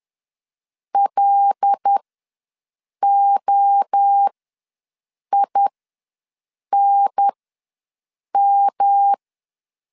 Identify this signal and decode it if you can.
Morse code, heard as a keyed tone, about 10 wpm, spaced as for 6 wpm, 800 Hz. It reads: LOINM
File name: LEZIONI / 5 LBVC / 5 LB ws 10wpm 3616